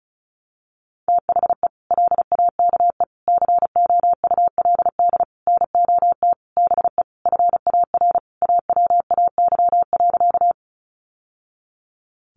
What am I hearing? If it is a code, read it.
Morse code, heard as a keyed tone, about 34 wpm, 700 Hz. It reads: THE LAKE COULD NOT BE FAR AWAY.
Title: Sentences from Top 1100 Words 35wpm